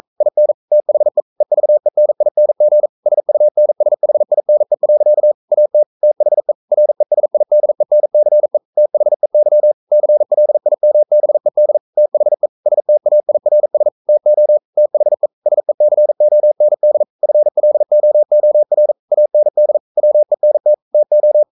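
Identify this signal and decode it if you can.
Morse code, heard as a keyed tone, about 42 wpm, 600 Hz. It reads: IN THE EVENING SUNSHINE. AT THE RESIDENCE THEY CLIMBED THE STAIRS TO THE SECOND FLOOR AND WENT TO
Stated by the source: Morse code sending of an English novel at 42 wpm.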